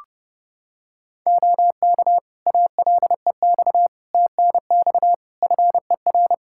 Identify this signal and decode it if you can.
Morse code, heard as a keyed tone, about 32 wpm, 700 Hz. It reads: OK ALEX TNX FER
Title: Practice QSO - Exchange 4 - Ending - 30wpm